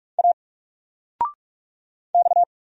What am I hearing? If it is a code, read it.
Morse code, heard as a keyed tone, about 44 wpm, 700 Hz. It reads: A X
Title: Single Letter-Number Alt 45wpm